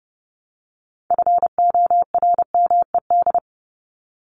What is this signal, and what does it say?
Morse code, heard as a keyed tone, about 30 wpm, 700 Hz. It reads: FORMED
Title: ED Suffix - Top 100 Words with Base Words 30wpm